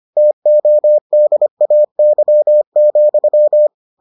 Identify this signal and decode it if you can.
Morse code, heard as a keyed tone, about 25 wpm, 600 Hz. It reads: TODAY,